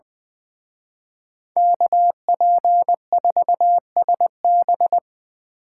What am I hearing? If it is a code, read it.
Morse code, heard as a keyed tone, about 25 wpm, 700 Hz. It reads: KP4SB
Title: Speed Racing - USA 2x2 Call Signs 20wpm